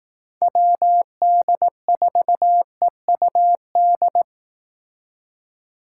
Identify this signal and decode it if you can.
Morse code, heard as a keyed tone, about 18 wpm, 700 Hz. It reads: WD4EUD
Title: Speed Racing - USA 2x3 Call Signs 15wpm